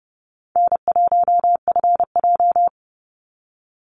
Morse code, heard as a keyed tone, about 30 wpm, 700 Hz. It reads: N1FJ
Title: NA SOTA Activator Callsigns 30wpm